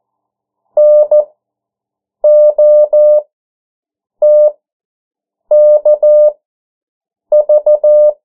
Morse code, heard as a keyed tone, about 14 wpm, spaced as for 7 wpm, 600 Hz. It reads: NOTKV
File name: morse mp3 / 09 urok 8